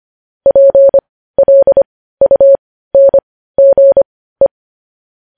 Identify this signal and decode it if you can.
Morse code, heard as a keyed tone, about 25 wpm, spaced as for 15 wpm, 550 Hz. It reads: PLUNGE
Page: news headlines in morse code at 15 wpm